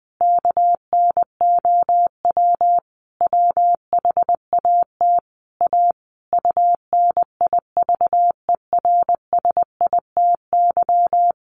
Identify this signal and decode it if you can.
Morse code, heard as a keyed tone, about 20 wpm, 700 Hz. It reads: KNOW WHAT A UNIVERSITY
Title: Sentences from Top 1500 Words 20wpm